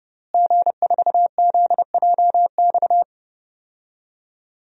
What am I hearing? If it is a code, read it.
Morse code, heard as a keyed tone, about 32 wpm, 700 Hz. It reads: G4ZJX
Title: DXCC-100 Call Signs - 5 Characters - Repeated 3x - 30wpm